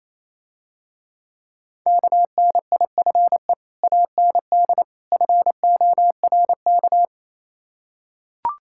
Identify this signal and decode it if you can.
Morse code, heard as a keyed tone, about 27 wpm, 690 Hz. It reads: KNIFE AND FORK
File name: Easy Words - Speed-Racing with Hints - 22wpm